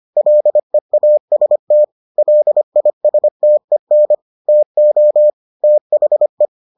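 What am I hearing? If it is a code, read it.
Morse code, heard as a keyed tone, about 25 wpm, 600 Hz. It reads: LEAST LISTEN TO THE
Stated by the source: Morse edition of the Foundations of Amateur Radio episodes.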